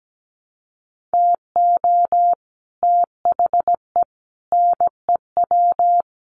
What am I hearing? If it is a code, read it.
Morse code, heard as a keyed tone, about 17 wpm, 700 Hz. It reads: TO THE NEW
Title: Sets of 5 Words Encore2 17wpm